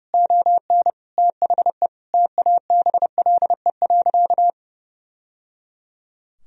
Morse code, heard as a keyed tone, about 30 wpm, 700 Hz. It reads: ON THE TABLE.
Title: Sentences from Top 1000 Words - Review Part 4 - 30wpm